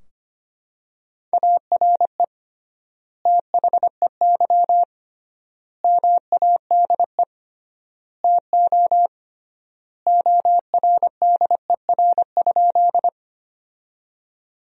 Morse code, heard as a keyed tone, about 25 wpm, 700 Hz. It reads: ARE THEY MADE TO ORDER?
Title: 3x Word-Spacing - Sentences from Top 400 Words - 25wpm